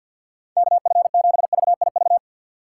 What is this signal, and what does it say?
Morse code, heard as a keyed tone, about 52 wpm, 700 Hz. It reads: KF7VIV
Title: USA Call Signs - 6 Characters - 4x Character Spacing - 50wpm